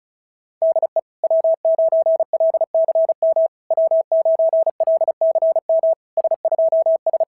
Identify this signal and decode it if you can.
Morse code, heard as a keyed tone, about 36 wpm, 690 Hz. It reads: DE W9LCM W9LCM S2S
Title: SOTA QSO Practice 35wpm